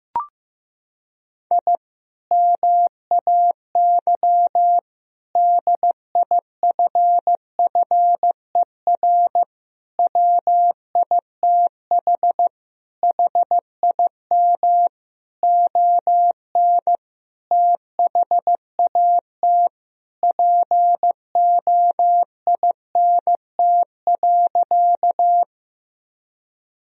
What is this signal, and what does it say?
Morse code, heard as a keyed tone, about 15 wpm, 700 Hz. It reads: I MAY DIFFER WITH HIM ON THAT POINT.
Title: Sentences from Top 1000 Words - Review Part 1 - 15wpm